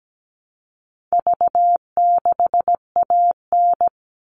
Morse code, heard as a keyed tone, about 17 wpm, 700 Hz. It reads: V6AN
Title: All Cut Numbers - 4-Digit 17wpm